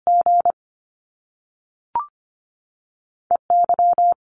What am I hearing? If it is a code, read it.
Morse code, heard as a keyed tone, about 25 wpm, spaced as for 3 wpm, 700 Hz. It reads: GE Y
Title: Top Two Letter Combinations in Words 25wpm